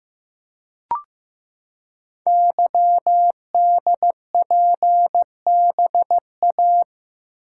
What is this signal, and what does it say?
Morse code, heard as a keyed tone, about 15 wpm, 700 Hz. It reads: YDPBA
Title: Code Groups - 5 Letters - 15wpm